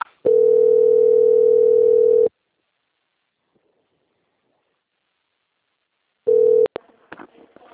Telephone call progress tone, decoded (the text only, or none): ringback tone